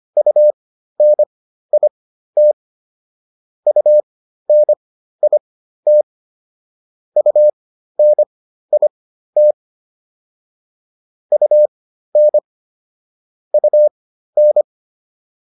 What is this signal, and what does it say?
Morse code, heard as a keyed tone, about 25 wpm, spaced as for 13 wpm, 600 Hz. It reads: UNIT UNIT UNIT UN UN